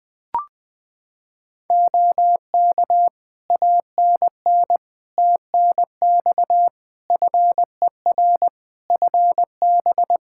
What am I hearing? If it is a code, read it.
Morse code, heard as a keyed tone, about 21 wpm, 700 Hz. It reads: OK ANN TNX FER FB
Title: Practice QSO - Exchange 4 - Ending - 20wpm